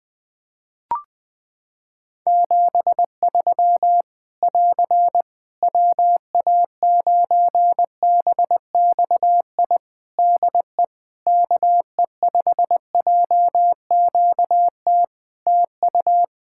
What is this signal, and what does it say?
Morse code, heard as a keyed tone, about 21 wpm, 700 Hz. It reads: 73 <AR> WA9BXI DE KE5JQT TU
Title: Practice QSO - Exchange 4 - Ending - 20wpm